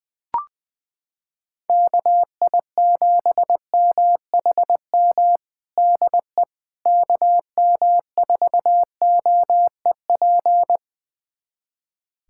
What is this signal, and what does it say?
Morse code, heard as a keyed tone, about 20 wpm, 700 Hz. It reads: KI7MHM DE KM4OEP
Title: USA Callsign DE Callsign - 6 Characters - 20wpm